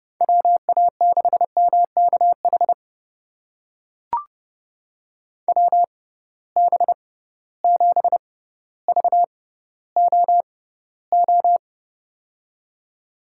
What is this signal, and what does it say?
Morse code, heard as a keyed tone, about 31 wpm, spaced as for 15 wpm, 700 Hz. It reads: WA6MKH WB7VOO
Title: USA Call Signs - 6 Characters - 6x Character Spacing - 30wpm